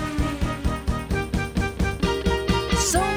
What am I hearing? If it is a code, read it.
There is no code.